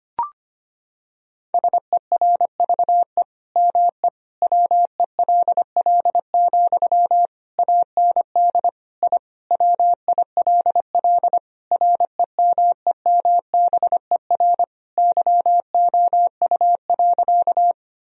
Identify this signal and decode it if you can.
Morse code, heard as a keyed tone, about 25 wpm, 700 Hz. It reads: SERVE ME WELL, AND I WILL REMEMBER YOU.
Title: Sentences from Top 1000 Words - Review Part 2 - 25wpm